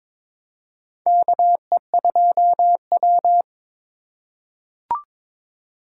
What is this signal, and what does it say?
Morse code, heard as a keyed tone, about 22 wpm, 700 Hz. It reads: KE2W
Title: USA Call Signs - 4 Characters - 3x Character Spacing - 22wpm